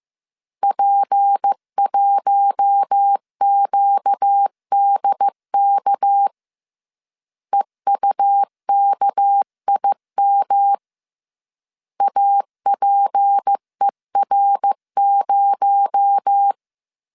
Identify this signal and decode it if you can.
Morse code, heard as a keyed tone, about 15 wpm, 800 Hz. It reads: P1QDK EUKIM APER0